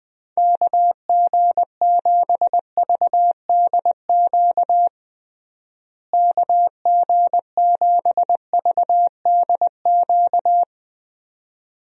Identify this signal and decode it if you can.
Morse code, heard as a keyed tone, about 21 wpm, 700 Hz. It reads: KG7VDQ KG7VDQ
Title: USA Call Signs - 6 Characters - Repeated 2x - 20wpm